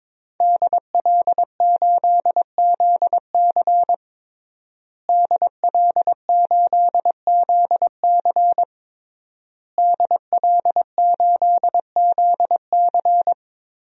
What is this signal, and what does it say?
Morse code, heard as a keyed tone, about 22 wpm, 700 Hz. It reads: DL8ZC DL8ZC DL8ZC
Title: DXCC-100 Call Signs - 5 Characters - Repeated 3x - 22wpm